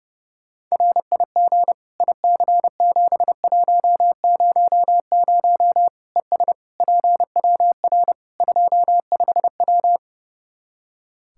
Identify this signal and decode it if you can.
Morse code, heard as a keyed tone, about 32 wpm, 700 Hz. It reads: RIG IC7100 ES PWR 25W
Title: K1USN SST QSO Practice 30wpm